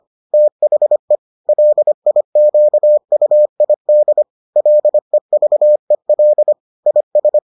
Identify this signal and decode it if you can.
Morse code, heard as a keyed tone, about 25 wpm, 600 Hz. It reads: THE LIQUID LEVEL IS